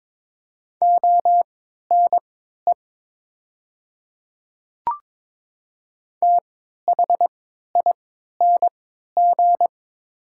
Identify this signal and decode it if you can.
Morse code, heard as a keyed tone, about 22 wpm, spaced as for 13 wpm, 700 Hz. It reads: ONE THING
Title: 3x Character Spacing - Top 100 Words - 22wpm